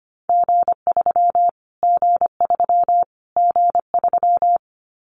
Morse code, heard as a keyed tone, about 25 wpm, 700 Hz. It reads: G3 G3 G3
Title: Callsign Prefixes for Top 25 DXCC - Repeated 3x - 25wpm